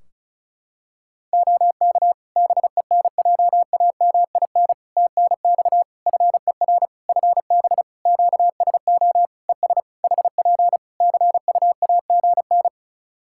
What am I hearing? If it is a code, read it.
Morse code, heard as a keyed tone, about 36 wpm, 700 Hz. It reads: OK BENJAMIN TNX FER FB QSO ES HP CUAGN
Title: K1USN SST QSO Practice 35wpm